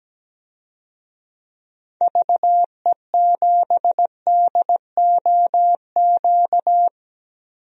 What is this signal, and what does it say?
Morse code, heard as a keyed tone, about 17 wpm, 700 Hz. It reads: VE7DOQ
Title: DXCC-100 Call Signs - 6 Characters - 17wpm